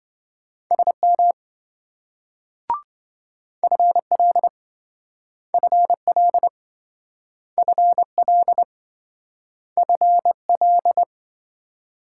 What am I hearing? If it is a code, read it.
Morse code, heard as a keyed tone, about 27 wpm, 700 Hz. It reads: SM FL FL FL FL